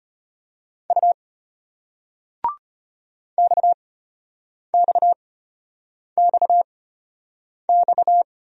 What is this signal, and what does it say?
Morse code, heard as a keyed tone, about 36 wpm, spaced as for 8 wpm, 700 Hz. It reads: U XXXX